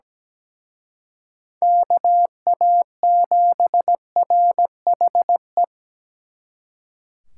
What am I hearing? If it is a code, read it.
Morse code, heard as a keyed tone, about 20 wpm, 700 Hz. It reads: KA7RHE
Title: Speed Racing - USA 2x3 Call Signs 17wpm